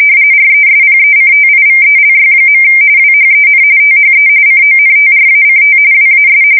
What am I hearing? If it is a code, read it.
RTTY, about 45.45 baud. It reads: RESPECTIVELY. WHEN THE CONTRACT WAS
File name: fsk170 / USS-New-Jersey